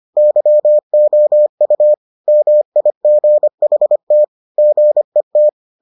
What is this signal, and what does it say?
Morse code, heard as a keyed tone, about 25 wpm, 600 Hz. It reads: YOU MIGHT GET